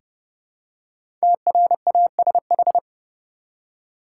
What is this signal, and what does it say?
Morse code, heard as a keyed tone, about 30 wpm, 700 Hz. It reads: TRASH
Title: Five Letter Words - 30wpm